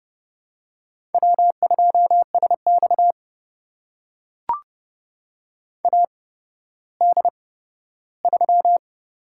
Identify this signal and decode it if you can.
Morse code, heard as a keyed tone, about 32 wpm, spaced as for 15 wpm, 700 Hz. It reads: W2SX AD3
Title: USA Call Signs - 4 Characters - 8x Character Spacing - 30wpm